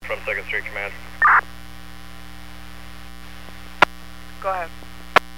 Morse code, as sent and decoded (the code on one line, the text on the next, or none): none